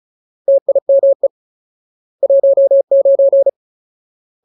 Morse code, heard as a keyed tone, about 34 wpm, 550 Hz. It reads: TIME 19
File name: Ultimate Word List / 35WPM Ultimate